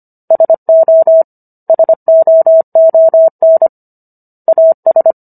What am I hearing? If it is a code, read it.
Morse code, heard as a keyed tone, about 25 wpm, 650 Hz. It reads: SO SOON AS